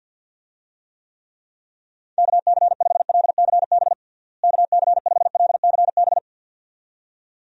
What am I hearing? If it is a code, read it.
Morse code, heard as a keyed tone, about 51 wpm, 700 Hz. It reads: KC5LCB KC5LCB
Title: USA Call Signs - 6 Characters - Repeated 2x - 50wpm